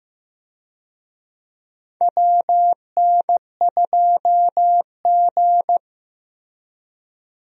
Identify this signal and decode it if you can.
Morse code, heard as a keyed tone, about 15 wpm, 700 Hz. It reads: WN2G
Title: USA Call Signs - 4 Characters - Repeated 2x - 15wpm